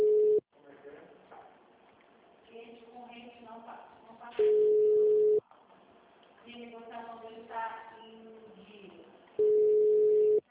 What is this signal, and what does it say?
Ringback tone (Europe, 425 Hz), with speech.